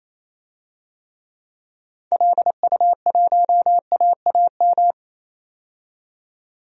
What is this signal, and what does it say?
Morse code, heard as a keyed tone, about 28 wpm, 700 Hz. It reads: LU1AAM
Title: DXCC-100 Call Signs - 6 Characters - 28wpm